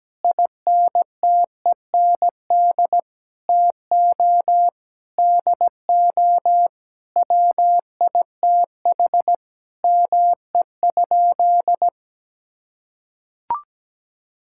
Morse code, heard as a keyed tone, about 17 wpm, 700 Hz. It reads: INTEND TO DO WITH ME?
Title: Sentences from Top 1600 Words 17wpm